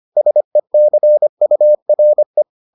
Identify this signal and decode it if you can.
Morse code, heard as a keyed tone, about 25 wpm, 600 Hz. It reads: SECURE